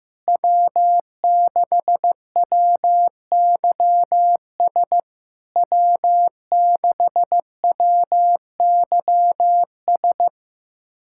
Morse code, heard as a keyed tone, about 15 wpm, 700 Hz. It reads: W6WYS W6WYS
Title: SOTA QSO Practice 15wpm